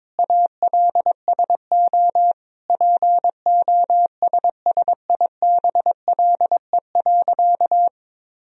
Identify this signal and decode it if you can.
Morse code, heard as a keyed tone, about 22 wpm, 700 Hz. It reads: ALSO POSSIBLE.
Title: Sentences from Top 1000 Words - Review Part 1 - 22wpm